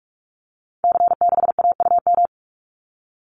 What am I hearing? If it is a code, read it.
Morse code, heard as a keyed tone, about 44 wpm, 700 Hz. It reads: C6AUM
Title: Call Signs Repeated 3x - 45wpm